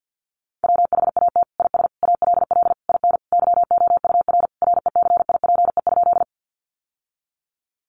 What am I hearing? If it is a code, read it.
Morse code, heard as a keyed tone, about 50 wpm, 700 Hz. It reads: WHAT IS ALL IN YOUR RECIPE?
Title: Sentences from Top 1200 Words 50wpm